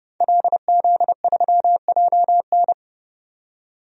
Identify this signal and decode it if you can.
Morse code, heard as a keyed tone, about 30 wpm, 700 Hz. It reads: LZ3JN
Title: DXCC-100 Call Signs - 5 Characters - 30wpm